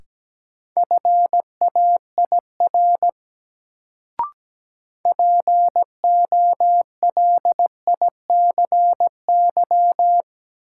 Morse code, heard as a keyed tone, about 17 wpm, 700 Hz. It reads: FAIR POLICY
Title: Sentences from Top 1000 Words - Review Part 5 - New Word Drill - 17wpm